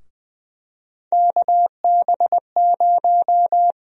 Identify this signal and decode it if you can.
Morse code, heard as a keyed tone, about 20 wpm, 700 Hz. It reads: KB0
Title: USA Call Sign Prefixes - Repeated 2x - 20wpm